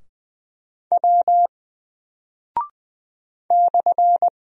Morse code, heard as a keyed tone, about 20 wpm, 700 Hz. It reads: W /
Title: Single Character (Letter:Number:Punctuation:Prosign) 20wpm